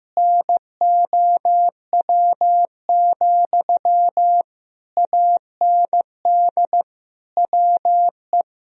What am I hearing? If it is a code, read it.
Morse code, heard as a keyed tone, about 15 wpm, 700 Hz. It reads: NOW, AND WE
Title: Sentences from Top 1000 Words - Review Part 4 - 15wpm